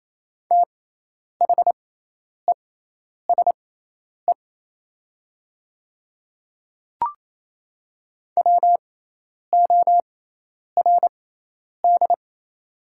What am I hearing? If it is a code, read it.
Morse code, heard as a keyed tone, about 28 wpm, spaced as for 10 wpm, 700 Hz. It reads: THESE WORD
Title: 6x Character Spacing - Top 100 Words - 28wpm